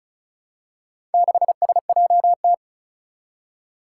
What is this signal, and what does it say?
Morse code, heard as a keyed tone, about 34 wpm, 700 Hz. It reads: 6SJT
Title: Four Challenging Letter-Number Combinations 35wpm